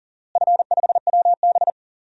Morse code, heard as a keyed tone, about 37 wpm, 700 Hz. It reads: F5WB
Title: DXCC-100 Call Signs - 4 Characters - Repeated 2x - 40wpm